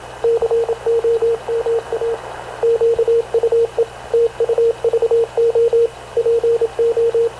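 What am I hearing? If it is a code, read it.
Morse code, heard as a keyed tone, about 26 wpm, 460 Hz, with radio noise. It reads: COMA QUE TUVO PO
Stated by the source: Havana Radio CLA press (PX) broadcast, recorded by Radio Officer Paul Dumesnil aboard mv CNAV QUEST/CZDO.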